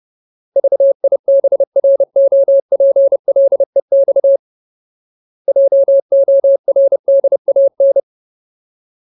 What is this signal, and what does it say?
Morse code, heard as a keyed tone, about 30 wpm, 550 Hz. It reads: VIBROPLEX JORDAN
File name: Ultimate Word List / 30WPM Ultimate